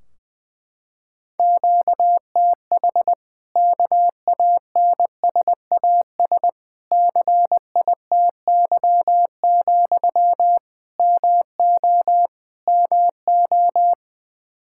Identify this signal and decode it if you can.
Morse code, heard as a keyed tone, about 20 wpm, 700 Hz. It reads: QTH KANSAS CITY, MO MO
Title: Practice QSO - Exchange 1 - Beginning - 20wpm